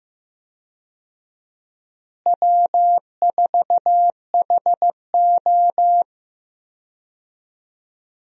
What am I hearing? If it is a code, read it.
Morse code, heard as a keyed tone, about 15 wpm, 700 Hz. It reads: W4HO
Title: USA Call Signs - 4 Characters - Repeated 2x - 15wpm